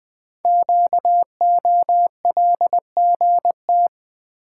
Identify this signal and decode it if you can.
Morse code, heard as a keyed tone, about 20 wpm, 700 Hz. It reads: QOLGT
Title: Code Groups - 5 Letters - 20wpm